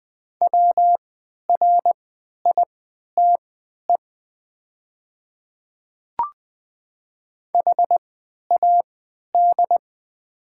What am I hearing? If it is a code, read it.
Morse code, heard as a keyed tone, about 20 wpm, spaced as for 11 wpm, 700 Hz. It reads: WRITE HAD
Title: 3x Character Spacing - Top 200 Words - 20wpm